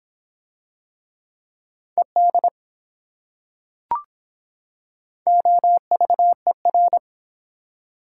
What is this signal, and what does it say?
Morse code, heard as a keyed tone, about 25 wpm, 700 Hz. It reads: ED OVER